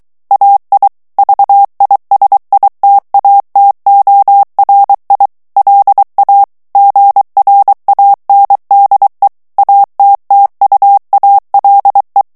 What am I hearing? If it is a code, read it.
Morse code, heard as a keyed tone, about 23 wpm, 800 Hz. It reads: AI VISITATORI LA GRANDE ATTUALE
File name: morse mp3 / ITALIEN texte clair 1 vitesse 22 mots minute